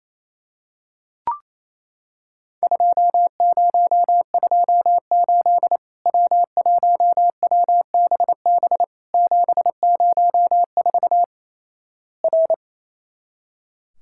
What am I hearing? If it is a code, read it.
Morse code, heard as a keyed tone, about 28 wpm, 700 Hz. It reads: 2028 W1WBB 704 R
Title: Worked All Europe DX Contest - QTC Exchange - 28wpm